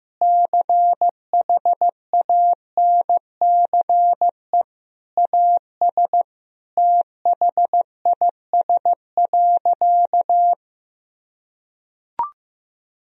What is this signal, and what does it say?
Morse code, heard as a keyed tone, about 15 wpm, 700 Hz. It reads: CHANCE AS THIS.
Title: Sentences from Top 1000 Words - Review Part 1 - 15wpm